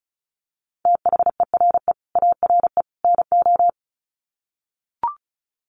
Morse code, heard as a keyed tone, about 34 wpm, 700 Hz. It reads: THERE ARE NO
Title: Sets of 3 Words 35wpm